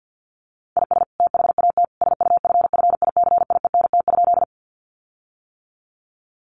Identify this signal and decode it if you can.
Morse code, heard as a keyed tone, about 50 wpm, 700 Hz. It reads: IS THAT SUFFICIENT?
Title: Sentences from Top 1400 Words 50wpm